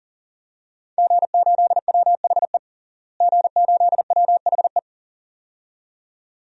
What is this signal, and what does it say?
Morse code, heard as a keyed tone, about 40 wpm, 700 Hz. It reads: G8WHE G8WHE
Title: DXCC-100 Call Signs - 5 Characters - Repeated 2x - 40wpm